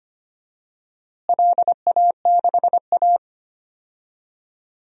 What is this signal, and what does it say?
Morse code, heard as a keyed tone, about 25 wpm, 700 Hz. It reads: LA6A
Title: DXCC-100 Call Signs - 4 Characters - Repeated 3x - 25wpm